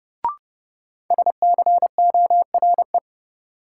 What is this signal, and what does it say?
Morse code, heard as a keyed tone, about 30 wpm, 700 Hz. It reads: SCORE
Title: Sentences from Top 1600 Words - New Word Drill - 30wpm